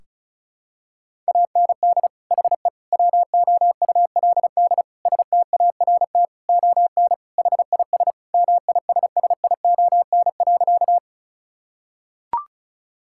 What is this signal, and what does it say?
Morse code, heard as a keyed tone, about 34 wpm, 700 Hz. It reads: AND HE WOULD START ON HIS MISSION.
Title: Sentences from Top 1600 Words 35wpm